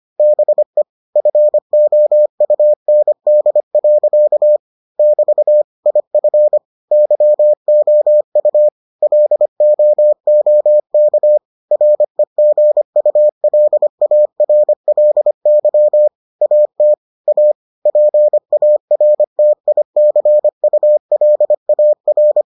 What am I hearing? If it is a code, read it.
Morse code, heard as a keyed tone, about 25 wpm, 600 Hz. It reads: BE FOUND. <BT> IF YOU LOOK REGULARLY AT A PARTICULAR